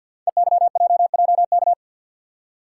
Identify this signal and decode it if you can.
Morse code, heard as a keyed tone, about 50 wpm, 700 Hz. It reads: EYJJK
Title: Code Groups - 5 Letters - 50wpm